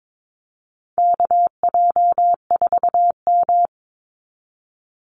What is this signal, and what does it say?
Morse code, heard as a keyed tone, about 22 wpm, 700 Hz. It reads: KJ4M
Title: Call Signs Repeated 2x - 22wpm